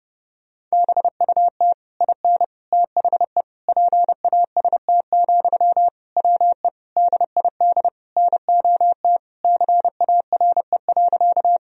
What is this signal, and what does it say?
Morse code, heard as a keyed tone, about 32 wpm, 700 Hz. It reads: BUT IN THE PAST, WE DID NOT CARE.
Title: Sentences from Top 1000 Words - Review Part 3 - 30wpm